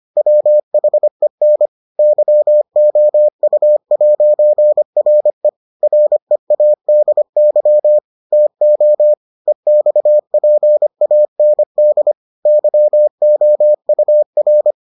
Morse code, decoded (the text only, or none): WHEN YOU'RE READY TO EXPAND YOUR